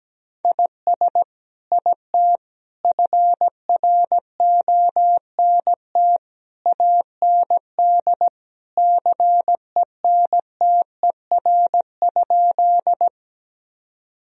Morse code, heard as a keyed tone, about 17 wpm, 700 Hz. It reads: IS IT FRONT AND CENTER?
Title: Sentences from Top 1000 Words - Review Part 2 - 17wpm